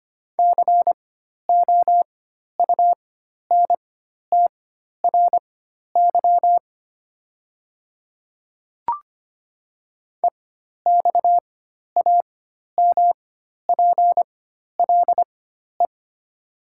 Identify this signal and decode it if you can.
Morse code, heard as a keyed tone, about 25 wpm, spaced as for 12 wpm, 700 Hz. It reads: COUNTRY EXAMPLE
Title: 4x Character Spacing - Top 300 Words - 25wpm